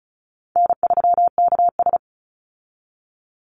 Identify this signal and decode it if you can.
Morse code, heard as a keyed tone, about 34 wpm, 700 Hz. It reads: N3KS
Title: Easy Call Signs Repeated 2x - 35wpm